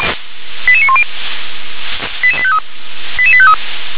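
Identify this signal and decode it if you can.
Five-tone selective call (ZVEI), heard as quick runs of five tones, tones of about 70 ms. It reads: 88910 88853 88853
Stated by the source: Five-tone ZVEI selective call received on SDR.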